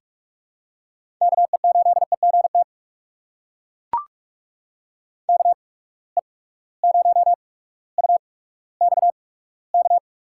Morse code, heard as a keyed tone, about 43 wpm, spaced as for 19 wpm, 700 Hz. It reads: KE8EGT KE0UXK
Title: USA Call Signs - 6 Characters - 8x Character Spacing - 45wpm